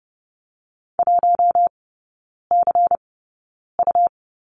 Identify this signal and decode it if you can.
Morse code, heard as a keyed tone, about 30 wpm, spaced as for 9 wpm, 700 Hz. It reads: 1CU
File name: Lesson 17e - Letter u - Three Characters 30wpm